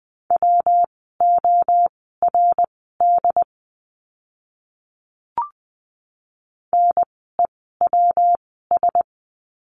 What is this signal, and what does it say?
Morse code, heard as a keyed tone, about 20 wpm, spaced as for 14 wpm, 700 Hz. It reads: WORD NEWS
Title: 115 Lesson 19f - Letter w - Words 20wpm